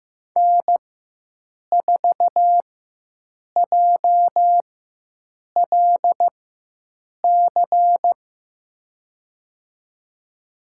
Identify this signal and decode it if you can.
Morse code, heard as a keyed tone, about 15 wpm, spaced as for 7 wpm, 700 Hz. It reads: N4JLC
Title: USA Call Signs - 5 Characters - 4x Character Spacing - 15wpm